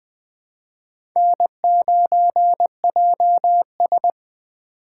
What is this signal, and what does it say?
Morse code, heard as a keyed tone, about 21 wpm, 700 Hz. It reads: N9JS
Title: USA Call Signs - 4 Characters - 2x Character Spacing - 20wpm